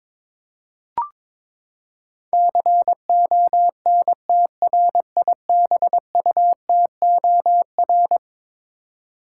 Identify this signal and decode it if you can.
Morse code, heard as a keyed tone, about 22 wpm, 700 Hz. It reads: CONTRIBUTOR
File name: practice / 22wpm / Eleven Letter Words - 22wpm